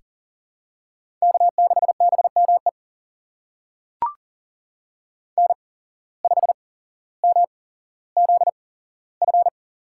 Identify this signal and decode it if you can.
Morse code, heard as a keyed tone, about 40 wpm, spaced as for 17 wpm, 700 Hz. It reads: K6BME N5MZF
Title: USA Call Signs - 5 Characters - 8x Character Spacing - 40wpm